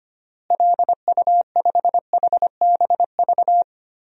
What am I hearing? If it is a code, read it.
Morse code, heard as a keyed tone, about 25 wpm, 700 Hz. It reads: LU5HBV
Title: DXCC-100 Call Signs - 6 Characters - Repeated 2x - 25wpm